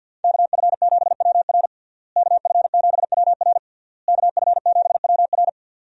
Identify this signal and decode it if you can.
Morse code, heard as a keyed tone, about 52 wpm, 700 Hz. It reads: KF7WR KF7WR KF7WR
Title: USA Call Signs - 5 Characters - Repeated 3x - 50wpm